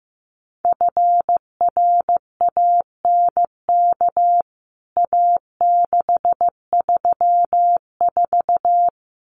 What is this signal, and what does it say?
Morse code, heard as a keyed tone, about 15 wpm, 700 Hz. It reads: FRANK A634
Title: CWT Exchange 15wpm